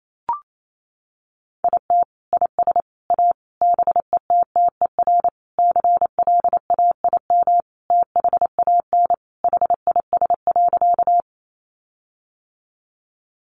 Morse code, heard as a keyed tone, about 28 wpm, 700 Hz. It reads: IT IS A BETTER CLAIM THAN HIS.
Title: Sentences from Top 1500 Words 28wpm